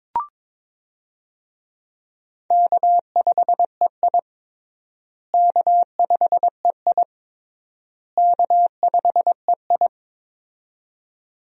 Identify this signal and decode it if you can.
Morse code, heard as a keyed tone, about 23 wpm, 700 Hz. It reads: K5EI K5EI K5EI
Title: USA Call Signs - 4 Characters - Repeated 3x - 22wpm